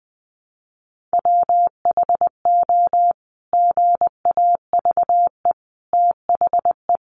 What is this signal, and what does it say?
Morse code, heard as a keyed tone, about 20 wpm, 700 Hz. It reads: WHO GAVE THE
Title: Sentences from Top 900 Words 20wpm